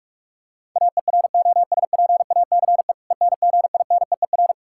Morse code, heard as a keyed tone, about 45 wpm, 700 Hz. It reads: AEROSPACE ENGINEER
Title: Practice QSO - Exchange 3 - More Info - 45wpm